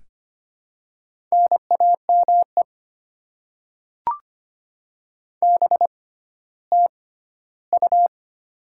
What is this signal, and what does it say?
Morse code, heard as a keyed tone, about 25 wpm, spaced as for 14 wpm, 700 Hz. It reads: NAME BTU
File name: Top 20 QSO Elements - 6x Character Spacing - 25wpm